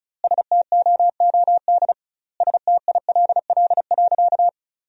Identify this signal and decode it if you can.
Morse code, heard as a keyed tone, about 36 wpm, 700 Hz. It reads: STOOD STILL.
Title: 2x Word-Spacing - Sentences from Top 500 Words - 35wpm